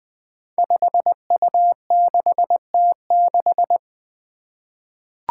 Morse code, heard as a keyed tone, about 20 wpm, 700 Hz. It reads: 5U6T6
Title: Five Challenging Letter-Number Combinations 20wpm